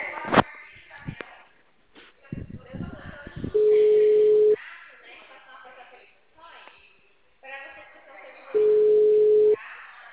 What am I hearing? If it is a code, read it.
Ringback tone (Europe, 425 Hz), with speech.
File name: out- -2044- - - .27440